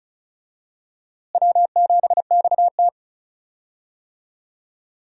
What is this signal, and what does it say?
Morse code, heard as a keyed tone, about 34 wpm, 700 Hz. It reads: W7XT
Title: USA Call Signs - 4 Characters - Repeated 2x - 35wpm